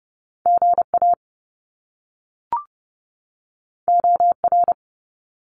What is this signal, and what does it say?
Morse code, heard as a keyed tone, about 30 wpm, 700 Hz. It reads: GA OR
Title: Top Two Letter Combinations in Words 30wpm